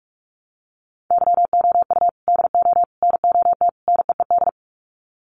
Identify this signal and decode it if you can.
Morse code, heard as a keyed tone, about 44 wpm, 700 Hz. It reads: YOU DO NOT NEED
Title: Sets of 4 Words 45wpm